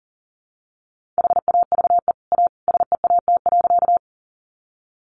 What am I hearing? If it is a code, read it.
Morse code, heard as a keyed tone, about 37 wpm, 700 Hz. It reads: HAVE A SEAT.
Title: Sentences from Top 1500 Words 40wpm